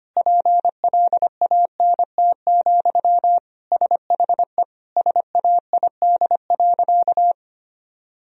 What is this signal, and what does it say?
Morse code, heard as a keyed tone, about 25 wpm, 700 Hz. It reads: PLANT, SHE SAID.
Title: Sentences from Top 1000 Words - Review Part 1 - 25wpm